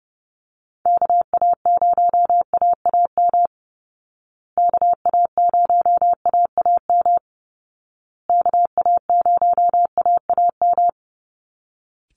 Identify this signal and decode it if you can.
Morse code, heard as a keyed tone, about 30 wpm, 700 Hz. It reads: KA0AAM KA0AAM KA0AAM
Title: Call Signs Repeated 2x - 30wpm